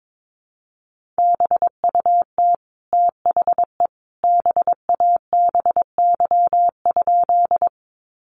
Morse code, heard as a keyed tone, about 22 wpm, 700 Hz. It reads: BUT THE BABY?
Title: Sentences from Top 1300 Words 22wpm